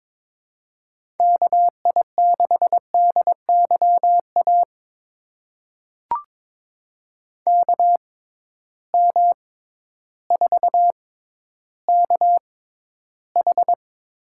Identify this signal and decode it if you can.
Morse code, heard as a keyed tone, about 22 wpm, spaced as for 12 wpm, 700 Hz. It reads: KI6DYA KM4KH
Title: USA Call Signs - 6 Characters - 6x Character Spacing - 22wpm